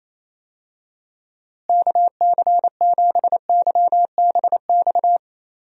Morse code, heard as a keyed tone, about 27 wpm, 700 Hz. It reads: KC7YBX
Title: USA Call Signs - 6 Characters - Repeated 3x - 28wpm